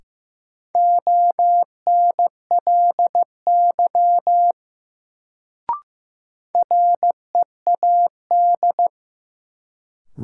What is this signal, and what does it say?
Morse code, heard as a keyed tone, about 15 wpm, 700 Hz. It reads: ONLY READ
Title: Sentences from Top 200 Words - New Word Drill - 15wpm